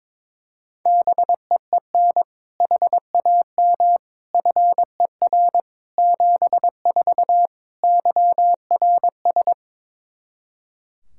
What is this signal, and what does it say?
Morse code, heard as a keyed tone, about 22 wpm, 700 Hz. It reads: BEEN HAM FER 74 YRS
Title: Practice QSO - Exchange 3 - More Info - 22wpm